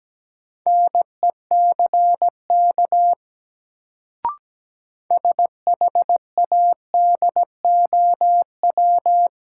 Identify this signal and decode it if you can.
Morse code, heard as a keyed tone, about 17 wpm, 700 Hz. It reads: NECK SHADOW
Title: Sets of 5 Words Encore - New Word Drill - 17wpm